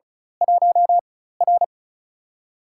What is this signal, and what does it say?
Morse code, heard as a keyed tone, about 33 wpm, 700 Hz. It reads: 1 R
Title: USA Call Signs - 4 Characters - 4x Character Spacing - 35wpm